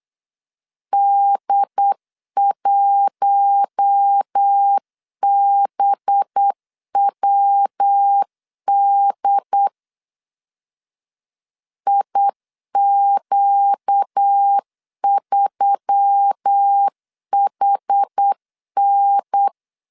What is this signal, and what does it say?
Morse code, heard as a keyed tone, about 8 wpm, 800 Hz. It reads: D1BWD IQ3HN
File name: CW / DET / 1 6 1500C S C 8wpm 5219